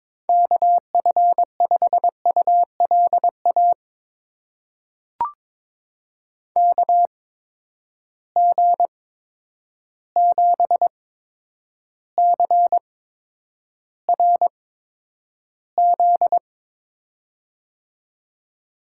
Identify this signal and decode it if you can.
Morse code, heard as a keyed tone, about 22 wpm, spaced as for 9 wpm, 700 Hz. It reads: KF5ULA KG7CRZ
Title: USA Call Signs - 6 Characters - 8x Character Spacing - 22wpm